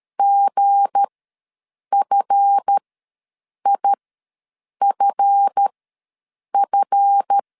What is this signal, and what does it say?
Morse code, heard as a keyed tone, about 13 wpm, spaced as for 7 wpm, 800 Hz. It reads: GFIFF
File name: LEZIONI / 7 GF9Z / 7 GF ws 12wpm 3140